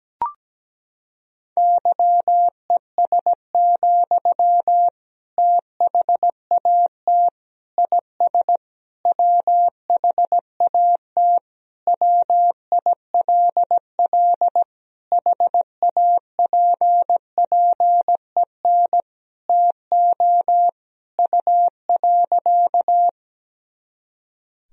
Morse code, heard as a keyed tone, about 17 wpm, 700 Hz. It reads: YES, THAT IS WHAT WILL HAPPEN TO U.
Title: Sentences from Top 500 Words and Abbreviations 17wpm